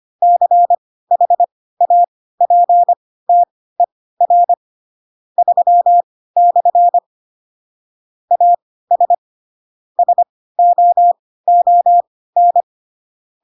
Morse code, heard as a keyed tone, about 25 wpm, spaced as for 16 wpm, 700 Hz. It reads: CHAPTER 3/ AS SOON
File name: Pinocchio / Pinocchio ch3 10 25 ik1hge